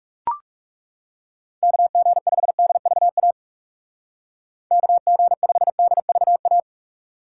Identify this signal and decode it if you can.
Morse code, heard as a keyed tone, about 36 wpm, 700 Hz. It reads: KG5DVA KG5DVA
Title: Speed Racing - USA 2x3 Call Signs 30wpm